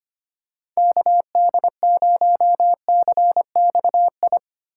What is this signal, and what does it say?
Morse code, heard as a keyed tone, about 25 wpm, 700 Hz. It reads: KD0CXI
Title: USA Call Signs - 6 Characters - 4x Character Spacing - 25wpm